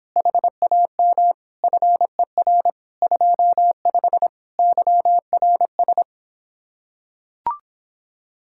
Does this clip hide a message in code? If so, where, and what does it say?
Yes: Morse code from 0.2 to 6.0 s, heard as a keyed tone, about 26 wpm, 700 Hz. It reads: HAM FER 25 YRS